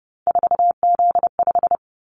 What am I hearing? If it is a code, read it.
Morse code, heard as a keyed tone, about 31 wpm, 700 Hz. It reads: 4Z5
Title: Callsign Prefixes for Top 50 DXCC - 30wpm